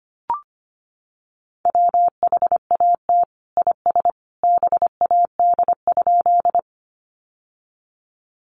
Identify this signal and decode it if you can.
Morse code, heard as a keyed tone, about 25 wpm, 700 Hz. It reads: WHAT IS BAD?
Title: Sentences from Top 700 Words 25wpm